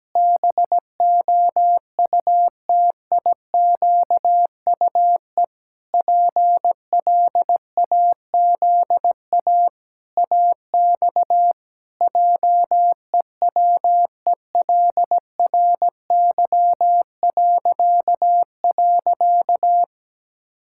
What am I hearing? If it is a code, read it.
Morse code, heard as a keyed tone, about 17 wpm, 700 Hz. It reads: BOUTIQUE PLAZA AX JEWELRY..
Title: Pangram Challenge v2 - 17wpm